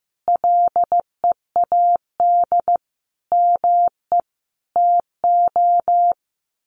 Morse code, heard as a keyed tone, about 15 wpm, 700 Hz. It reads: LEAD ME TO
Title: Sentences from Top 1200 Words 15wpm